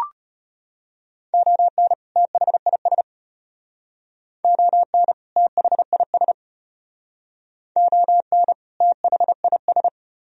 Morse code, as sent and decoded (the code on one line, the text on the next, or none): --- -. / - .... .. ... / --- -. / - .... .. ... / --- -. / - .... .. ...
ON THIS ON THIS ON THIS